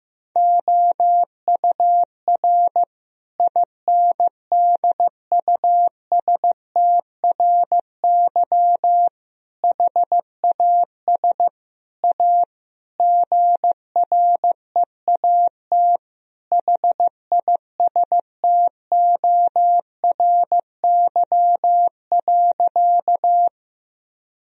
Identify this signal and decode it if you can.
Morse code, heard as a keyed tone, about 15 wpm, 700 Hz. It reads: OUR INDUSTRY HAS A GREAT HISTORY.
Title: Sentences from Top 1000 Words - Review Part 4 - 15wpm